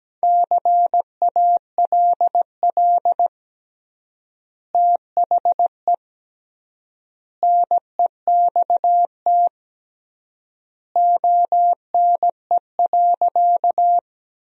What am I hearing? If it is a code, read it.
Morse code, heard as a keyed tone, about 17 wpm, 700 Hz. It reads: CALL THE NEXT ONE.
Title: 3x Word-Spacing - Sentences from Top 300 Words - 17wpm